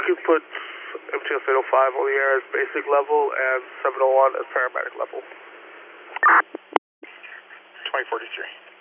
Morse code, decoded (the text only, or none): none